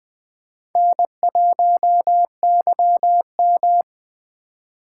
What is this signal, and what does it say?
Morse code, heard as a keyed tone, about 20 wpm, 700 Hz. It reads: N1YM
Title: Speed Racing - USA 1x2 Call Signs 17wpm